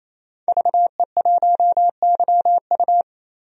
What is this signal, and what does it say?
Morse code, heard as a keyed tone, about 28 wpm, 700 Hz. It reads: VE1YU
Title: DXCC-100 Call Signs - 5 Characters - 28wpm